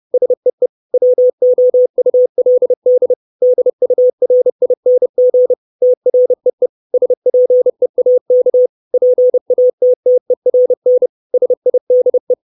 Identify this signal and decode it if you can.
Morse code, heard as a keyed tone, about 30 wpm, 500 Hz. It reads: SEE WOULD DURING TREE SPEAK PATTERN SIDE
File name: 632 Words 30WPM